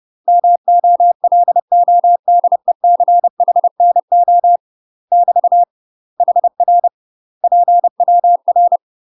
Morse code, heard as a keyed tone, about 30 wpm, 700 Hz. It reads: MOLODECHNO <BT> HR PWR
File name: UK MASTER QSOs TRAINING/30WPM